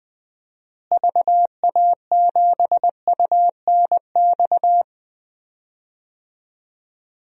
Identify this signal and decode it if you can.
Morse code, heard as a keyed tone, about 21 wpm, 700 Hz. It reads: VA7UNX
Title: DXCC-100 Call Signs - 6 Characters - Repeated 2x - 20wpm